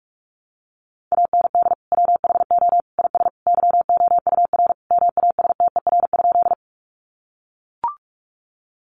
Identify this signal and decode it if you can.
Morse code, heard as a keyed tone, about 44 wpm, 700 Hz. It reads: AND WHO IS YOUR MASTER?
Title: Sentences from Top 1300 Words 45wpm